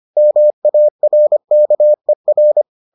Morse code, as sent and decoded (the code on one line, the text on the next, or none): -- .- .-. -.- . .-.
MARKER